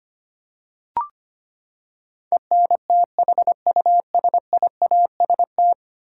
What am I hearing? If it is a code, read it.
Morse code, heard as a keyed tone, about 25 wpm, 700 Hz. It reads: ENTHUSIAST
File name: Ten Letter Words - 25wpm